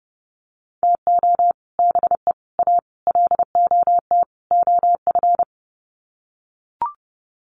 Morse code, heard as a keyed tone, about 30 wpm, 700 Hz. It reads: TO BE A LOT OF
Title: Sets of 5 Words Encore 30wpm